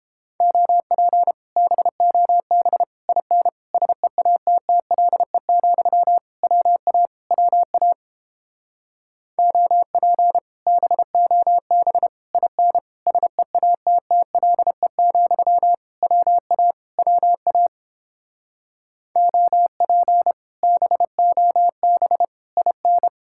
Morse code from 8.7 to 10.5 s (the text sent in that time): OP